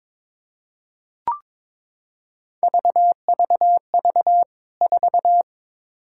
Morse code, heard as a keyed tone, about 21 wpm, 700 Hz. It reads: VVV 4
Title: K1USN SST QSO Practice 22wpm